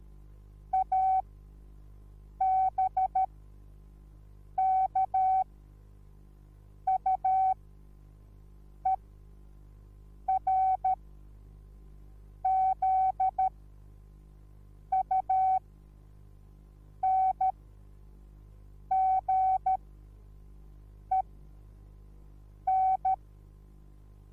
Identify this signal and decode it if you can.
Morse code, heard as a keyed tone, about 13 wpm, spaced as for 5 wpm, 760 Hz. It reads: ABKUERZUNGEN